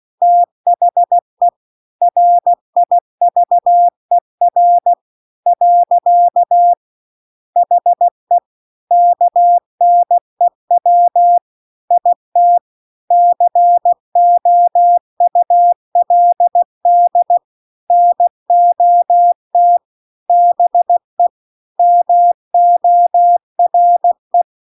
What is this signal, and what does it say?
Morse code, heard as a keyed tone, about 16 wpm, 700 Hz. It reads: THE RIVER . HE KNEW IT COULD NOT BE MORE